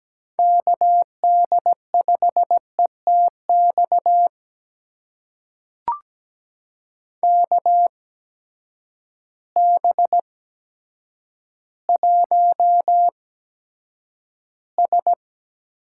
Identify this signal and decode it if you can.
Morse code, heard as a keyed tone, about 17 wpm, spaced as for 8 wpm, 700 Hz. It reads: KD5ETX KB1S
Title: USA Call Signs - 6 Characters - 8x Character Spacing - 17wpm